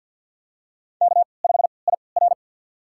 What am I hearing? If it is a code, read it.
Morse code, heard as a keyed tone, about 47 wpm, spaced as for 28 wpm, 690 Hz. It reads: K5IR
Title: USA Call Signs - 3x Character Spacing - 50wpm